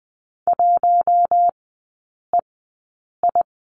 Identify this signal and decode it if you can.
Morse code, heard as a keyed tone, about 20 wpm, spaced as for 8 wpm, 700 Hz. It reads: 1EI
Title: 047 Lesson 9e - Number 4 - Three Characters 20wpm